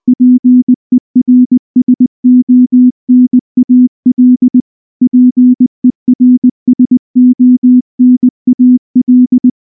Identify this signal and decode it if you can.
Morse code, heard as a keyed tone, about 19 wpm, 250 Hz. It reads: PERSONAL PERSONAL
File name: morse / morse-training-20WPM-1